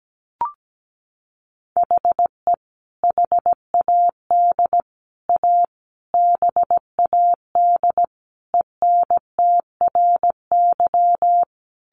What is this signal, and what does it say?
Morse code, heard as a keyed tone, about 17 wpm, 700 Hz. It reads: HE HAD A BAD ENTRY
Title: Sentences from Top 1500 Words 17wpm